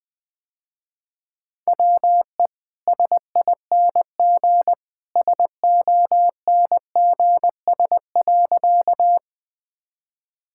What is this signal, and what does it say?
Morse code, heard as a keyed tone, about 20 wpm, 700 Hz. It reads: WE SING SONGS.
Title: Easy Sentences with Hints - 20wpm